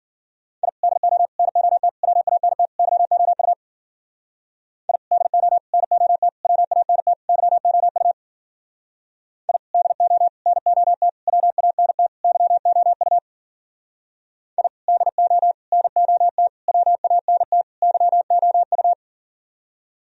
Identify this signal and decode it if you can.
Morse code, heard as a keyed tone, about 56 wpm, 700 Hz. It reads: I DO NOT WANT YOU I DO NOT WANT YOU I DO NOT WANT YOU I DO NOT WANT YOU